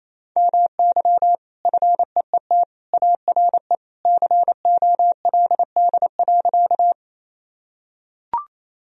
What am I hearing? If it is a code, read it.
Morse code, heard as a keyed tone, about 28 wpm, 700 Hz. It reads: MY FEET ARE COLD.
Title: Sentences from Top 1000 Words - Review Part 1 - 28wpm